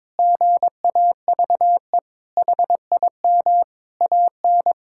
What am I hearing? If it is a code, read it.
Morse code, heard as a keyed tone, about 22 wpm, 700 Hz. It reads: GAVE HIM AN
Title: Sentences from Top 1000 Words - Review Part 5 - 22wpm